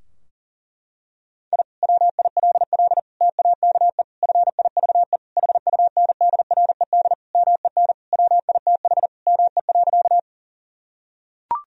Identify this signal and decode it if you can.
Morse code, heard as a keyed tone, about 40 wpm, 700 Hz. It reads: I WILL TAKE FIVE HUNDRED MEN WITH ME.
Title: Sentences from Top 1000 Words - Review Part 2 - 40wpm